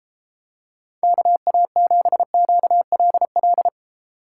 Morse code, heard as a keyed tone, about 33 wpm, 700 Hz. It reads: KA7QLL